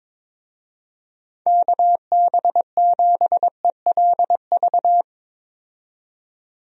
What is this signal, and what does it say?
Morse code, heard as a keyed tone, about 22 wpm, 700 Hz. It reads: KB7ELV
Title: USA Call Signs - 6 Characters - 22wpm